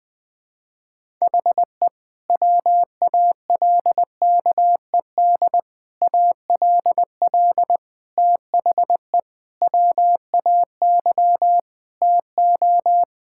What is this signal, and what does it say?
Morse code, heard as a keyed tone, about 21 wpm, 700 Hz. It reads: HE WALKED ALL THE WAY TO
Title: Sentences with 5-Word Phrases - Spoken Beginning - 20wpm